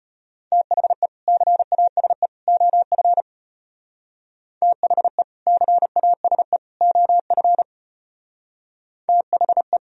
Morse code, heard as a keyed tone, about 31 wpm, 700 Hz. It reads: THE CASE OF THE CASE OF THE
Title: Speed-Racing - Sets of 3 Words 25wpm